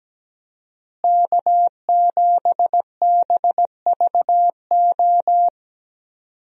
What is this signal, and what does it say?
Morse code, heard as a keyed tone, about 17 wpm, 700 Hz. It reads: K7BVO
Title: USA Call Signs - 5 Characters - Repeated 2x - 17wpm